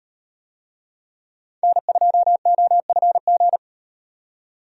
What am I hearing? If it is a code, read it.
Morse code, heard as a keyed tone, about 38 wpm, 700 Hz. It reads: N2OFG